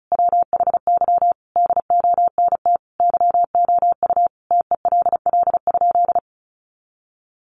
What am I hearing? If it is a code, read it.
Morse code, heard as a keyed tone, about 34 wpm, 700 Hz. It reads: WHY DONT YOU TELL?
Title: Sentences from Top 300 Words 35wpm